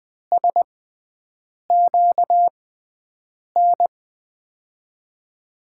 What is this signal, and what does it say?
Morse code, heard as a keyed tone, about 20 wpm, spaced as for 7 wpm, 700 Hz. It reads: SQN
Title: USA Call Signs - 6 Characters - 6x Character Spacing - 20wpm